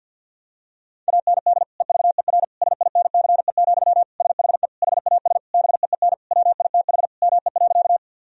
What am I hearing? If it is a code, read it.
Morse code, heard as a keyed tone, about 49 wpm, 700 Hz. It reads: AND EVER SINCE, SHE HAS BEEN WITH ME.
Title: Sentences from Top 1000 Words - Review Part 1 - 50wpm